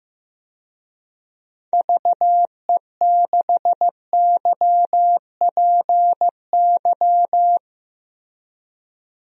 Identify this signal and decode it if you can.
Morse code, heard as a keyed tone, about 15 wpm, 700 Hz. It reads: VE6YPY
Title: DXCC-100 Call Signs - 6 Characters - Repeated 2x - 15wpm